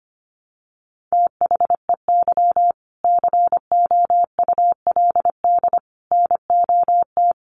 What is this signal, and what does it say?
Morse code, heard as a keyed tone, about 25 wpm, 700 Hz. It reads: THEY COULD NOT
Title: Sentences from Top 1100 Words 25wpm